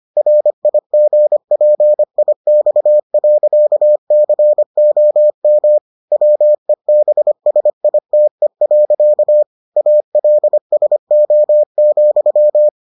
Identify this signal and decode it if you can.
Morse code, heard as a keyed tone, about 25 wpm, 600 Hz. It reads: RIGPIX.COM WEBSITE. ALSO,